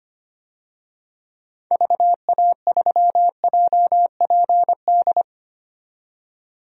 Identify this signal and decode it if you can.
Morse code, heard as a keyed tone, about 25 wpm, 700 Hz. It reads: VA3JPD
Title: DXCC-100 Call Signs - 6 Characters - 25wpm